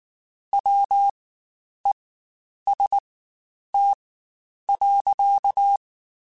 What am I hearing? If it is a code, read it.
Morse code, heard as a keyed tone, about 19 wpm, spaced as for 9 wpm, 800 Hz. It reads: WEST.
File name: Spaced Characters / Spaced Characters-mp3s / Weather 5 1 sp @18